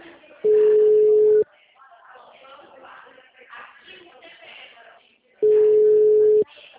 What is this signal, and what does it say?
Ringback tone (Europe, 425 Hz), with speech.